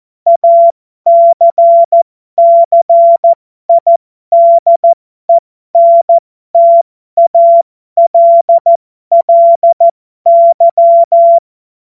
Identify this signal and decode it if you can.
Morse code, heard as a keyed tone, about 14 wpm, spaced as for 12 wpm, 680 Hz. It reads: ACCIDENTALLY
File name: Morsekurs / Witze